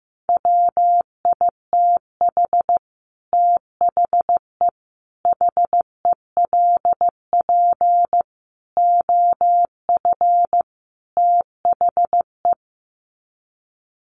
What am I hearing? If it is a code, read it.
Morse code, heard as a keyed tone, about 15 wpm, 700 Hz. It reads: WITH THE HELP OF THE
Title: Sets of 5 Words Encore 15wpm